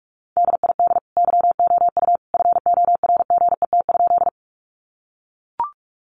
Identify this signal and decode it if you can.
Morse code, heard as a keyed tone, about 44 wpm, 700 Hz. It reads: DID YOU FORGET?
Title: Sentences from Top 1200 Words 45wpm